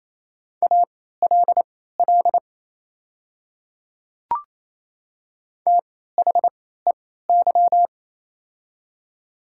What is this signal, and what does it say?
Morse code, heard as a keyed tone, about 28 wpm, spaced as for 16 wpm, 700 Hz. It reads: ALL THEY
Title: 3x Character Spacing - Top 200 Words - 28wpm